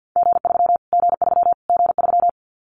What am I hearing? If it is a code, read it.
Morse code, heard as a keyed tone, about 48 wpm, 700 Hz. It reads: G3 G3 G3
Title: Callsign Prefixes for Top 25 DXCC - Repeated 3x - 50wpm